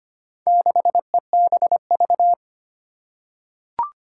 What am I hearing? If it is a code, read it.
Morse code, heard as a keyed tone, about 25 wpm, 700 Hz. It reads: 6EBV
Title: Four Challenging Letter-Number Combinations 25wpm